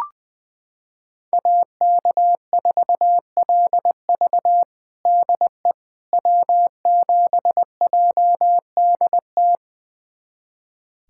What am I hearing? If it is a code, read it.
Morse code, heard as a keyed tone, about 21 wpm, 700 Hz. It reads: AK4LV DE W7JDT
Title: USA Callsign DE Callsign - 5 Characters - 20wpm